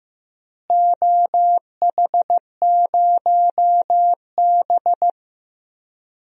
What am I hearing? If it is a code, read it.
Morse code, heard as a keyed tone, about 15 wpm, 700 Hz. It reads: OH0B
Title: DXCC-100 Call Signs - 4 Characters - Repeated 3x - 15wpm